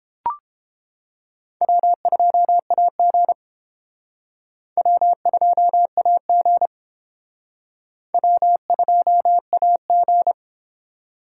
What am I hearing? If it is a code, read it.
Morse code, heard as a keyed tone, about 29 wpm, 700 Hz. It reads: W2AG W2AG W2AG